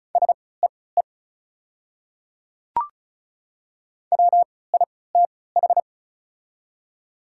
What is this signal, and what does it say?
Morse code, heard as a keyed tone, about 36 wpm, spaced as for 20 wpm, 700 Hz. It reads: SEE WITH
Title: 3x Character Spacing - Top 100 Words - 35wpm